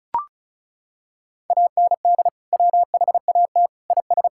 Morse code, heard as a keyed tone, about 36 wpm, 700 Hz. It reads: AND WHAT IS
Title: Sentences from Top 1000 Words - Review Part 6 - 35wpm